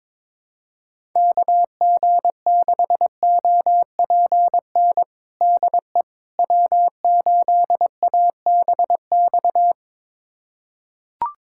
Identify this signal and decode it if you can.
Morse code, heard as a keyed tone, about 21 wpm, 700 Hz. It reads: KG6OPN DE W8ABX
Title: USA Callsign DE Callsign 22wpm